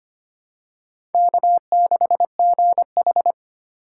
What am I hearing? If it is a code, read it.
Morse code, heard as a keyed tone, about 25 wpm, 700 Hz. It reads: K6GH
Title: USA Call Signs - 4 Characters - 3x Character Spacing - 25wpm